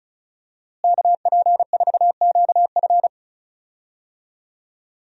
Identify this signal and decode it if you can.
Morse code, heard as a keyed tone, about 36 wpm, 700 Hz. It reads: KP4QF
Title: DXCC-100 Call Signs - 5 Characters - Repeated 2x - 35wpm